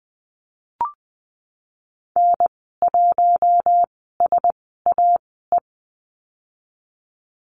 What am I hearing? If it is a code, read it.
Morse code, heard as a keyed tone, about 20 wpm, spaced as for 14 wpm, 700 Hz. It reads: N1SAE
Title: 042 Lesson 8g - Number 1 - Call Signs 20wpm